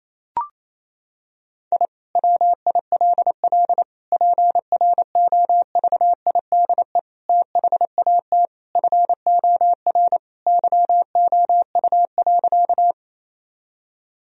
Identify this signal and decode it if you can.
Morse code, heard as a keyed tone, about 28 wpm, 700 Hz. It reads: I WILL PROVIDE THAT FOR YOU.
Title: Sentences from Top 1000 Words - Review Part 3 - 28wpm